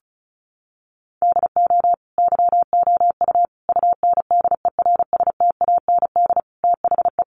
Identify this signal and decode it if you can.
Morse code, heard as a keyed tone, about 34 wpm, 700 Hz. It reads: DO YOU UNDERSTAND THE
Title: Sentences from Top 1200 Words 35wpm